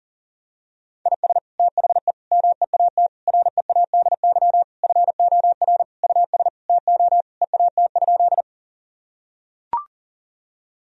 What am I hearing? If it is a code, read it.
Morse code, heard as a keyed tone, about 37 wpm, 700 Hz. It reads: IS THE MEAT READY FOR US TO EAT?
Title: Sentences from Top 1000 Words - Review Part 3 - 40wpm